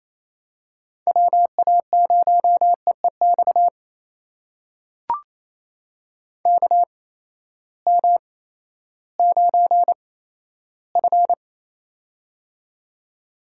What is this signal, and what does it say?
Morse code, heard as a keyed tone, about 28 wpm, spaced as for 14 wpm, 700 Hz. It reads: WA0EEX KM9F
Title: USA Call Signs - 8x Character Spacing - 28wpm